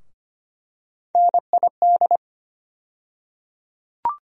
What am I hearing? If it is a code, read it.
Morse code, heard as a keyed tone, about 25 wpm, 700 Hz. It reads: NID
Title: USA Call Sign Postfixes - 3 Letters 25wpm